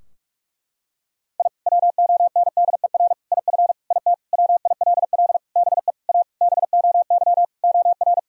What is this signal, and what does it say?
Morse code, heard as a keyed tone, about 46 wpm, 700 Hz. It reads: I WONDER IF IT WILL BE A BOY OR
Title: Sentences from Top 1000 Words - Review Part 3 - 45wpm